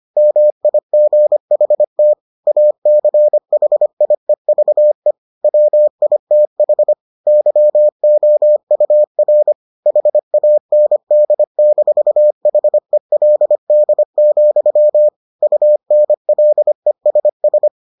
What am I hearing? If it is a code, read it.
Morse code, heard as a keyed tone, about 25 wpm, 600 Hz. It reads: MIGHT ACHIEVE WITH YOUR HAND-HELD, UNLESS